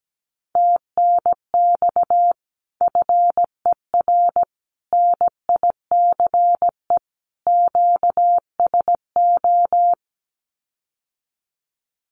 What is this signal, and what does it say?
Morse code, heard as a keyed tone, about 17 wpm, 700 Hz. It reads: TNX FER NICE QSO
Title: Common QSO Phrases Encore 17wpm